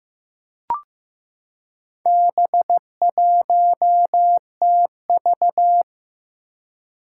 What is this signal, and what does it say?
Morse code, heard as a keyed tone, about 15 wpm, 700 Hz. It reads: B1TV
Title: Four Challenging Letter-Number Combinations 15wpm